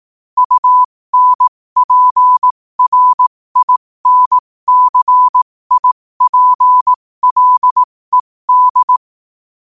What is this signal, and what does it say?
Morse code, heard as a keyed tone, about 18 wpm, spaced as for 15 wpm, 1000 Hz. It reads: UNPRINCIPLED
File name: qotdfiles / 15wpm